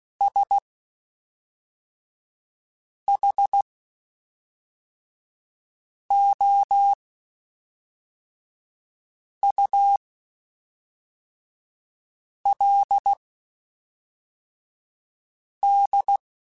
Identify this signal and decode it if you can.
Morse code, heard as a keyed tone, about 16 wpm, spaced as for 3 wpm, 800 Hz. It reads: SHOULD
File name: Spaced / Chuckles Spaced-mp3s / Temptation sp @5